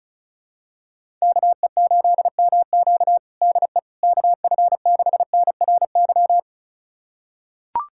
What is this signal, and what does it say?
Morse code, heard as a keyed tone, about 34 wpm, 700 Hz. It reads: KE8MQ DE KF6NRY
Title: USA Callsign DE Callsign 35wpm